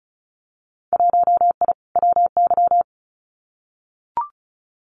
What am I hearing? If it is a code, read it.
Morse code, heard as a keyed tone, about 36 wpm, 700 Hz. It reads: 1I WY
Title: Winter Field Day Exchanges Encore 35wpm